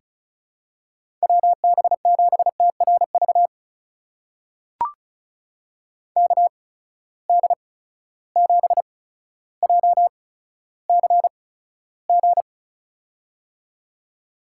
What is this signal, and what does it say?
Morse code, heard as a keyed tone, about 34 wpm, spaced as for 15 wpm, 700 Hz. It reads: WB7TRV KD7JCG
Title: USA Call Signs - 6 Characters - 8x Character Spacing - 35wpm